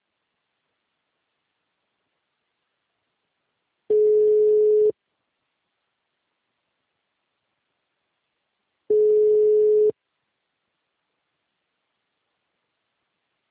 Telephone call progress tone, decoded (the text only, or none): ringback tone